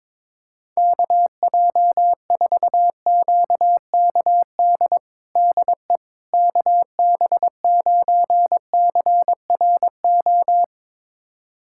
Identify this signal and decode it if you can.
Morse code, heard as a keyed tone, about 22 wpm, 700 Hz. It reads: KJ4QKD DE KB9CRO
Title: USA Callsign DE Callsign - 6 Characters - 22wpm